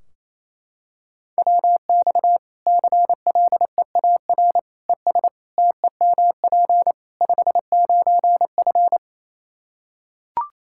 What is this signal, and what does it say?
Morse code, heard as a keyed tone, about 28 wpm, 700 Hz. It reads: WX CLEAR ES TEMP 59F
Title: K1USN SST QSO Practice 28wpm